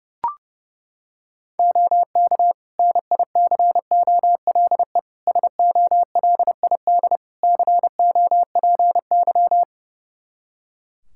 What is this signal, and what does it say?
Morse code, heard as a keyed tone, about 32 wpm, 700 Hz. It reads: OK NICOLE SOLID COPY
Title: Practice QSO - Exchange 3 - More Info - 30wpm